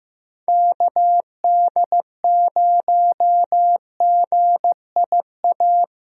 Morse code, heard as a keyed tone, about 15 wpm, 700 Hz. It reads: KD0GIA
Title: USA Call Signs - 6 Characters - Repeated 3x - 15wpm